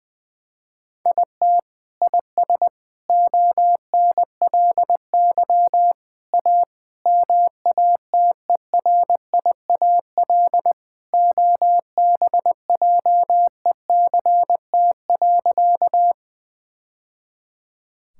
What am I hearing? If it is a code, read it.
Morse code, heard as a keyed tone, about 21 wpm, 700 Hz. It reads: IT IS ONLY A MATERIAL OBJECT.
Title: Sentences from Top 1000 Words - Review Part 5 - 20wpm